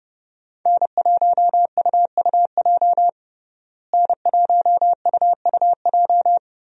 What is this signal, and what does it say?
Morse code, heard as a keyed tone, about 32 wpm, 700 Hz. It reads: N1UUJ N1UUJ
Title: USA Call Signs - 5 Characters - Repeated 3x - 30wpm